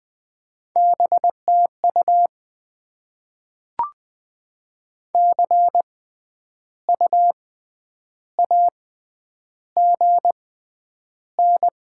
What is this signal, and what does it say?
Morse code, heard as a keyed tone, about 20 wpm, spaced as for 9 wpm, 700 Hz. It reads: BTU CU AGN
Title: Top 20 QSO Elements - 6x Character Spacing - 20wpm